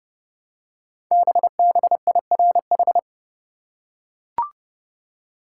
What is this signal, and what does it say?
Morse code, heard as a keyed tone, about 30 wpm, 700 Hz. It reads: BBIRH
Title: Code Groups - 5 Letters - 30wpm